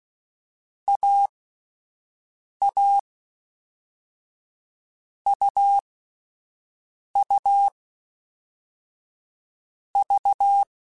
Morse code, heard as a keyed tone, about 16 wpm, 800 Hz. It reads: A A U U V